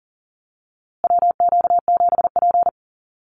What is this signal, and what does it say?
Morse code, heard as a keyed tone, about 40 wpm, 700 Hz. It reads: WQ7P
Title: USA Call Signs Repeated 2x 40wpm